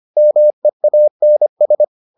Morse code, heard as a keyed tone, about 25 wpm, 600 Hz. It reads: MEANS